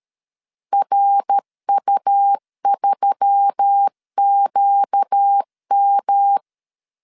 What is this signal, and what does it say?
Morse code, heard as a keyed tone, about 13 wpm, 800 Hz. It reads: RU3QM
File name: LEZIONI / 6 PKX0 / 6 P 12wpm 2033